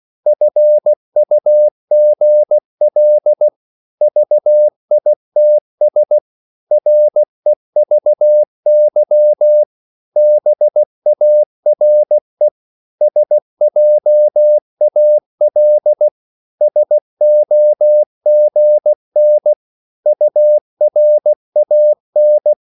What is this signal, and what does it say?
Morse code, heard as a keyed tone, about 16 wpm, 600 Hz. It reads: FUGL VITS REVY BARE SJAL SOGN URAN
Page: Korte danske ord 16wpm